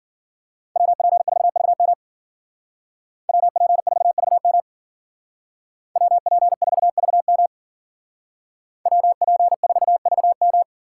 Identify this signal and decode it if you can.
Morse code, heard as a keyed tone, about 52 wpm, 700 Hz. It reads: WP4VM WP4VM WP4VM WP4VM